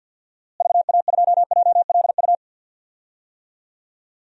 Morse code, heard as a keyed tone, about 46 wpm, 700 Hz. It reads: VA2JLU
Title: DXCC-100 Call Signs - 6 Characters - Repeated 3x - 50wpm